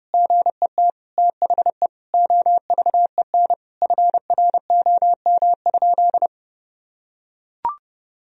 Morse code, heard as a keyed tone, about 32 wpm, 700 Hz. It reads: GET THE OVEN FROM?
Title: Sentences from Top 1000 Words - Review Part 4 - 30wpm